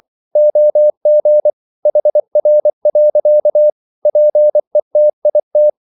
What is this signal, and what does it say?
Morse code, heard as a keyed tone, about 24 wpm, 600 Hz. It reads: OG HR. PETIT
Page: Ferie 24 wpm